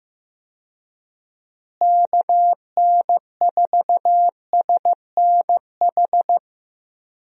Morse code, heard as a keyed tone, about 16 wpm, 700 Hz. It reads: KN4SNH
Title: USA Call Signs - 6 Characters - Repeated 2x - 15wpm